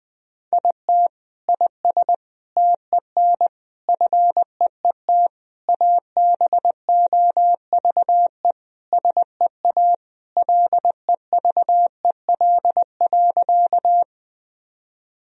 Morse code, heard as a keyed tone, about 20 wpm, 700 Hz. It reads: IT IS TEN FEET ABOVE SEA LEVEL.
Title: Sentences from Top 1000 Words - Review Part 4 - 20wpm